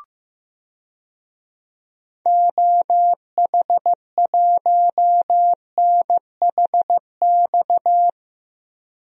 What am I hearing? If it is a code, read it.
Morse code, heard as a keyed tone, about 15 wpm, 700 Hz. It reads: OH1NHX
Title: DXCC-100 Call Signs - 6 Characters - Repeated 3x - 15wpm